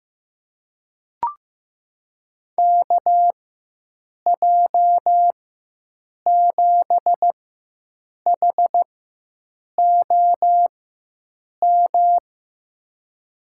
Morse code, heard as a keyed tone, about 15 wpm, spaced as for 7 wpm, 700 Hz. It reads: KJ7HOM
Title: USA Call Signs - 4x Character Spacing - 15wpm